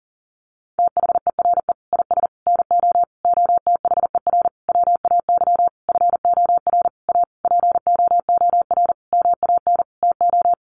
Morse code, heard as a keyed tone, about 40 wpm, 700 Hz. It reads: THERE IS NO OTHER WAY FOR A POOR MAN TO
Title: Sentences from Top 1300 Words 40wpm